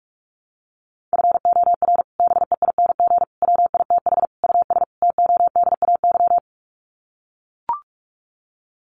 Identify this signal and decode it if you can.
Morse code, heard as a keyed tone, about 44 wpm, 700 Hz. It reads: FOR BEING WITH US TODAY
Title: Sets of 5 Words Encore2 45wpm